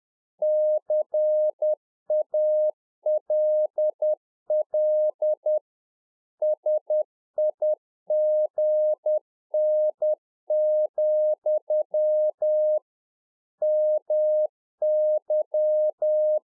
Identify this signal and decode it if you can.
Morse code, heard as a keyed tone, about 10 wpm, 610 Hz. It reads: CALL SIGN, MY
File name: UNFOGETTABLE QSO 10WPM